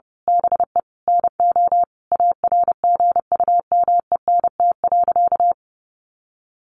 Morse code, heard as a keyed tone, about 28 wpm, 700 Hz. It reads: BE NO ARGUMENT.
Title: A Sound Of Thunder 30wpm